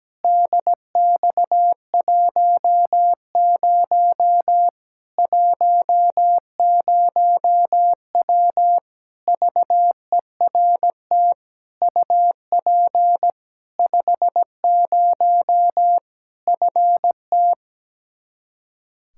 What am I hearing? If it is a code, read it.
Morse code, heard as a keyed tone, about 17 wpm, 700 Hz. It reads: DX10 10W VERT UP 50 FT
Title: Practice QSO - Exchange 2 - Info - 17wpm